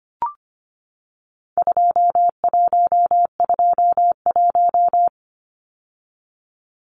Morse code, heard as a keyed tone, about 25 wpm, 700 Hz. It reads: 2121
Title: Numbers - 4-Digit 25wpm